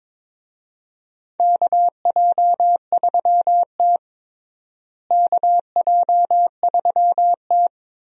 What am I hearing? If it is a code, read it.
Morse code, heard as a keyed tone, about 22 wpm, 700 Hz. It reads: KJ3T KJ3T
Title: USA Call Signs - 4 Characters - Repeated 3x - 22wpm